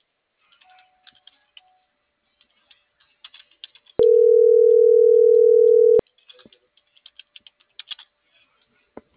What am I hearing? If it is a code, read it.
Ringback tone (North America), 440 + 480 Hz.